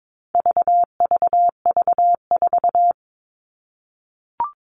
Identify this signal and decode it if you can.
Morse code, heard as a keyed tone, about 22 wpm, 700 Hz. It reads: VVV4
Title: All Cut Numbers - 4-Digit 22wpm